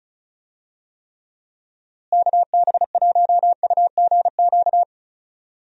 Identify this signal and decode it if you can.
Morse code, heard as a keyed tone, about 36 wpm, 700 Hz. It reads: KB1UGQ
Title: USA Call Signs - 6 Characters - Repeated 3x - 35wpm